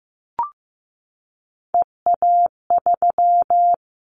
Morse code, heard as a keyed tone, about 15 wpm, 700 Hz. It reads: EA3
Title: Callsign Prefixes for Top 25 DXCC - Repeated 3x - 15wpm